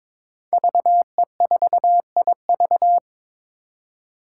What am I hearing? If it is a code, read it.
Morse code, heard as a keyed tone, about 22 wpm, 700 Hz. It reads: VE4IV
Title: Five Challenging Letter-Number Combinations 22wpm